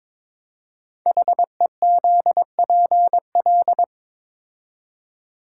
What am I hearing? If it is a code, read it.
Morse code, heard as a keyed tone, about 23 wpm, 700 Hz. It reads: HEZPL
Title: Code Groups - 5 Letters - 22wpm